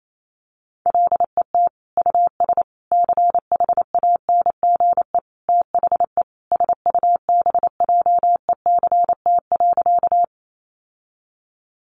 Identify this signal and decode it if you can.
Morse code, heard as a keyed tone, about 28 wpm, 700 Hz. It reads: LET US CHANGE THE SUBJECT.
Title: Sentences from Top 800 Words 28wpm